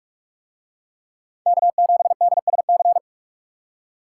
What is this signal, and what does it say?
Morse code, heard as a keyed tone, about 41 wpm, 700 Hz. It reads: K7DSC